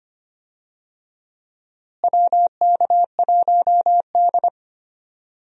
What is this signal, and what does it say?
Morse code, heard as a keyed tone, about 25 wpm, 700 Hz. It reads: WK1D
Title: USA Call Signs - 4 Characters - Repeated 2x - 25wpm